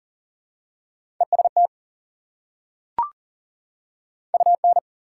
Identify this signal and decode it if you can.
Morse code, heard as a keyed tone, about 40 wpm, 700 Hz. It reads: EST UN
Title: Prefixes and Suffixes 40wpm